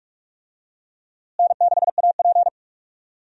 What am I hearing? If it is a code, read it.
Morse code, heard as a keyed tone, about 46 wpm, 700 Hz. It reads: N6AP
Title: USA Call Signs - 4 Characters - Repeated 2x - 45wpm